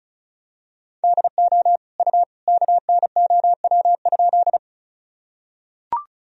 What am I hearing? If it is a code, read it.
Morse code, heard as a keyed tone, about 36 wpm, 700 Hz. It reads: DO U KNOW?
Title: Sentences from Top 100 Words and Abbreviations 35wpm